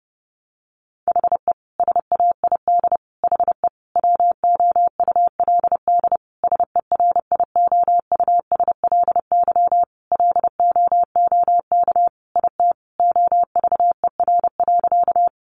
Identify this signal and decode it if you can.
Morse code, heard as a keyed tone, about 30 wpm, 700 Hz. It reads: HE SAID HE WOULD SERIOUSLY LOOK IT OVER.
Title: Sentences from Top 1200 Words 30wpm